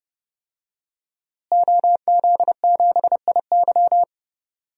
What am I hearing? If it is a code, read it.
Morse code, heard as a keyed tone, about 31 wpm, 700 Hz. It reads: OZ7IY
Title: DXCC-100 Call Signs - 5 Characters - Repeated 3x - 30wpm